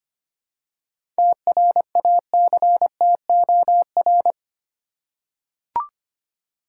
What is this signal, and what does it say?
Morse code, heard as a keyed tone, about 25 wpm, 700 Hz. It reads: TRACTOR
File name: practice / 25wpm / Easy Words with Hints - 25wpm